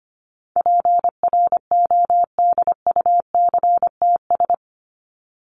Morse code, heard as a keyed tone, about 25 wpm, 700 Hz. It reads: PRODUCTS
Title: ES:S Suffix - Top 100 Words with Base Words 25wpm